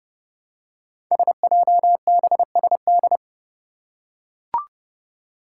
Morse code, heard as a keyed tone, about 30 wpm, 700 Hz. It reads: SJBSD
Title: Five Challenging Letter-Number Combinations 30wpm